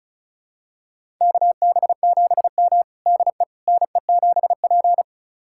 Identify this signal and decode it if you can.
Morse code, heard as a keyed tone, about 34 wpm, 700 Hz. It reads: KB7M DE NE7P
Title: USA Callsign DE Callsign - 4 Characters - 35wpm